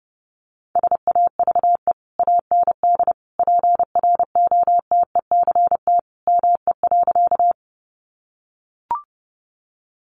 Morse code, heard as a keyed tone, about 29 wpm, 700 Hz. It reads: SAVE AND PROTECT ME.
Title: Sentences from Top 1100 Words 30wpm